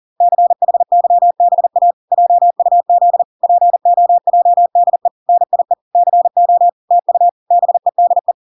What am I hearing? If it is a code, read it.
Morse code, heard as a keyed tone, about 40 wpm, 700 Hz. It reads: CHYBA JUZ POJDE NIE CO TU BEDE